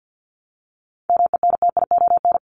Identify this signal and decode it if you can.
Morse code, heard as a keyed tone, about 50 wpm, 700 Hz. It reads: MENTION
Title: Top 1501-1750 Words 50wpm